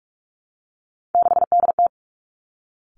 Morse code, heard as a keyed tone, about 46 wpm, 700 Hz. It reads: 6DT
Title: Three Challenging Letter-Number Combinations 45wpm